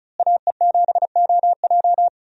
Morse code, heard as a keyed tone, about 34 wpm, 700 Hz. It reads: AE7OJ
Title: USA Call Signs - 5 Characters - Repeated 3x - 35wpm